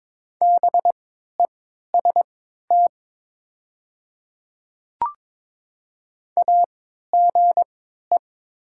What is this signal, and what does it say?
Morse code, heard as a keyed tone, about 22 wpm, spaced as for 13 wpm, 700 Hz. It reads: BEST AGE
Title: 3x Character Spacing - Top 500 Words - 22wpm